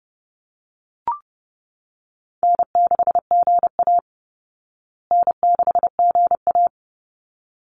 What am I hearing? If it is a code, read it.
Morse code, heard as a keyed tone, about 30 wpm, 700 Hz. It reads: N6GA N6GA
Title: Easy Call Signs Repeated 2x - 30wpm